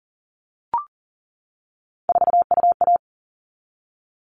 Morse code, heard as a keyed tone, about 40 wpm, 700 Hz. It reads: 4UA
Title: All Cut Numbers - 3-Digit 40wpm